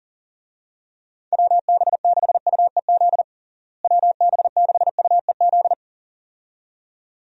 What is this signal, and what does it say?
Morse code, heard as a keyed tone, about 40 wpm, 700 Hz. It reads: WB6UEZ WB6UEZ
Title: USA Call Signs - 6 Characters - Repeated 2x - 40wpm